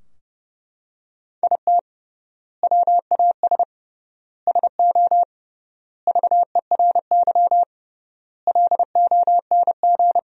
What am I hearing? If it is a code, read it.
Morse code, heard as a keyed tone, about 28 wpm, 700 Hz. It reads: IT WAS SO VERY LONG
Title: 3x Word-Spacing - Sentences from Top 500 Words - 30wpm